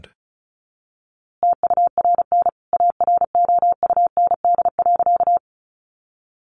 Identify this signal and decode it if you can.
Morse code, heard as a keyed tone, about 34 wpm, 700 Hz. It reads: TURN AROUND.
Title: A Sound Of Thunder 35wpm